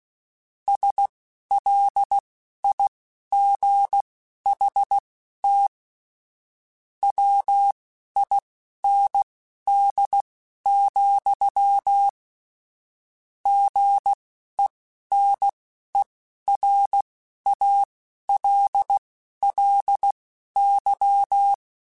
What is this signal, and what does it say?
Morse code, heard as a keyed tone, about 16 wpm, spaced as for 12 wpm, 800 Hz. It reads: SLIGHT WIND, GENERALLY